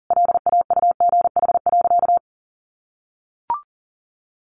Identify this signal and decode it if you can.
Morse code, heard as a keyed tone, about 40 wpm, 700 Hz. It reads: LAUGH.
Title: Sentences from Top 1500 Words 40wpm